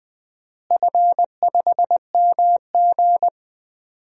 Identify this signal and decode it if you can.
Morse code, heard as a keyed tone, about 19 wpm, 700 Hz. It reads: F5MG
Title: DXCC-100 Call Signs - 4 Characters - Repeated 2x - 20wpm